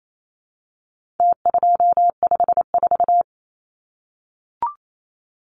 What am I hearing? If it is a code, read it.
Morse code, heard as a keyed tone, about 28 wpm, 700 Hz. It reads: T254
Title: Numbers - Contest Style - 4-Digit 28wpm